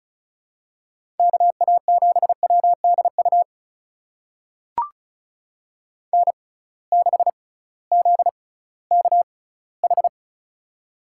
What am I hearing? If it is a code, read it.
Morse code, heard as a keyed tone, about 34 wpm, spaced as for 19 wpm, 700 Hz. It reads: KA7WDU N6ZKH
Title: USA Call Signs - 6x Character Spacing - 35wpm